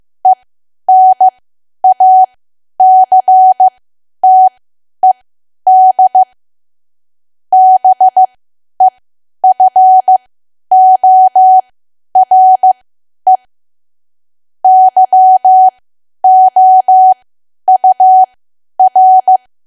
Morse code, heard as a keyed tone, about 15 wpm, spaced as for 10 wpm, 750 Hz. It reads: ENACTED BEFORE YOUR
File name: cwa / W1AW-Level2 / 10WPM